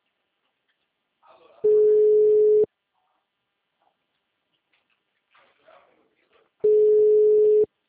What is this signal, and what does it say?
Ringback tone (Europe, 425 Hz).